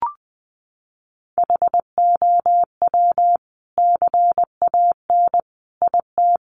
Morse code, heard as a keyed tone, about 20 wpm, 700 Hz. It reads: HOW CAN IT
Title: Sentences from Top 1200 Words 20wpm